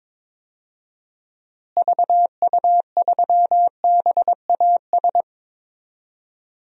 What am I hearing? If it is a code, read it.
Morse code, heard as a keyed tone, about 22 wpm, 700 Hz. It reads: VU3BAS
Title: DXCC-100 Call Signs - 6 Characters - Repeated 2x - 22wpm